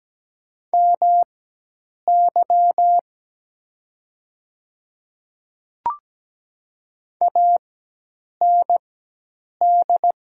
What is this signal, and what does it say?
Morse code, heard as a keyed tone, about 17 wpm, spaced as for 8 wpm, 700 Hz. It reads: MY AND
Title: 4x Character Spacing - Top 200 Words - 17wpm